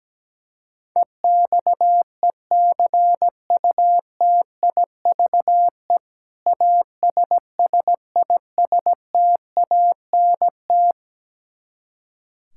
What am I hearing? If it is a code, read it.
Morse code, heard as a keyed tone, about 17 wpm, 700 Hz. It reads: EXECUTIVE ASSISTANT
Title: Practice QSO - Exchange 3 - More Info - 17wpm